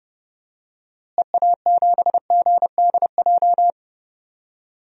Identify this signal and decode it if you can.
Morse code, heard as a keyed tone, about 30 wpm, 700 Hz. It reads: EA7GDJ
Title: DXCC-100 Call Signs - 6 Characters - Repeated 3x - 30wpm